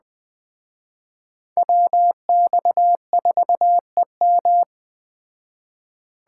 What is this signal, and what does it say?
Morse code, heard as a keyed tone, about 21 wpm, 700 Hz. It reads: WX4EM